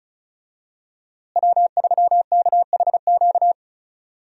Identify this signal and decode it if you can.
Morse code, heard as a keyed tone, about 34 wpm, 700 Hz. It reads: W3KHQ
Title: USA Call Signs - 5 Characters - Repeated 2x - 35wpm